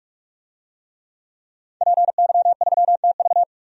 Morse code, heard as a keyed tone, about 44 wpm, 700 Hz. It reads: PY3TV
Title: DXCC-100 Call Signs - 5 Characters - 45wpm